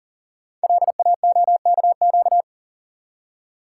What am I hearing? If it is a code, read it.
Morse code, heard as a keyed tone, about 38 wpm, 700 Hz. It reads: LAOKQ
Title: Code Groups - 5 Letters - 40wpm